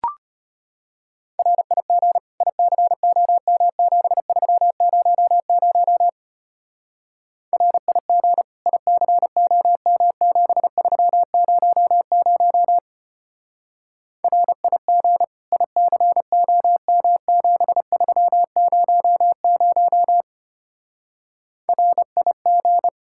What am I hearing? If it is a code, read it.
Morse code, heard as a keyed tone, about 32 wpm, 700 Hz. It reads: RIG ICOM7300 RIG ICOM7300 RIG ICOM7300 RIG